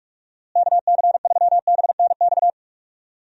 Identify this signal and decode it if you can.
Morse code, heard as a keyed tone, about 46 wpm, 700 Hz. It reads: KC3BNX
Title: USA Call Signs - 6 Characters - Repeated 2x - 45wpm